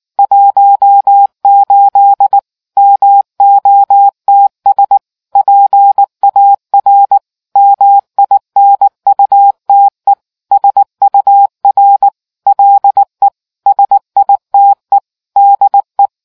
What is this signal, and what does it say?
Morse code, heard as a keyed tone, about 19 wpm, 800 Hz. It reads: 18 MOTS PAR MINUTE SUR LE SITE DE